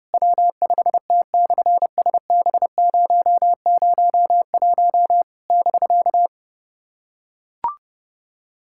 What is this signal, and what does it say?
Morse code, heard as a keyed tone, about 27 wpm, 700 Hz. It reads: W5T/SB001 <BK>
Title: SOTA QSO Practice 30wpm